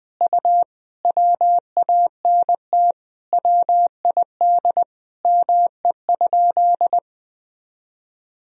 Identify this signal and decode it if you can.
Morse code, heard as a keyed tone, about 20 wpm, 700 Hz. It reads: U WANT WID ME?
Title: Sentences from Top 200 Words and Abbreviations 20wpm